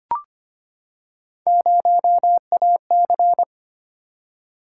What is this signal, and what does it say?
Morse code, heard as a keyed tone, about 25 wpm, 700 Hz. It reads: 0AC
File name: Three Letter-Number Combinations - 25wpm